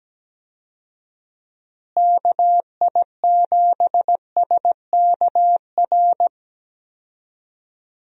Morse code, heard as a keyed tone, about 17 wpm, 700 Hz. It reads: KI7SKR
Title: USA Call Signs - 6 Characters - Repeated 2x - 17wpm